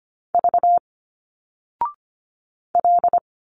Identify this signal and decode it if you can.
Morse code, heard as a keyed tone, about 25 wpm, 700 Hz. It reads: V L
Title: Single Letters 25wpm